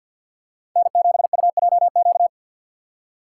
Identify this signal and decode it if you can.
Morse code, heard as a keyed tone, about 53 wpm, 700 Hz. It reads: N7UJQ
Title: USA Call Signs - 5 Characters - 2x Character Spacing - 50wpm